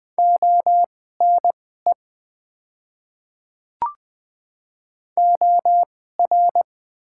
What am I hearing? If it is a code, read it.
Morse code, heard as a keyed tone, about 20 wpm, spaced as for 14 wpm, 700 Hz. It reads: ONE OR
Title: 2x Character Spacing - Top 200 Words - 20wpm